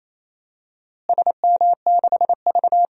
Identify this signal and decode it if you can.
Morse code, heard as a keyed tone, about 28 wpm, 700 Hz. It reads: SM6V
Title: DXCC-100 Call Signs - 4 Characters - Repeated 3x - 28wpm